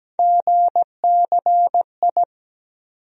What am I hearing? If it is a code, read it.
Morse code, heard as a keyed tone, about 17 wpm, 700 Hz. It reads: GCI
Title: USA Call Sign Postfixes - 3 Letters 17wpm